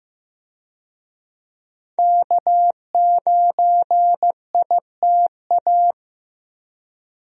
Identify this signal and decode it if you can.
Morse code, heard as a keyed tone, about 15 wpm, 700 Hz. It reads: K9ITA
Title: USA Call Signs - 5 Characters - Repeated 2x - 15wpm